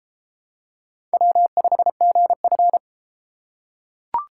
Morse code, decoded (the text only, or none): W5GF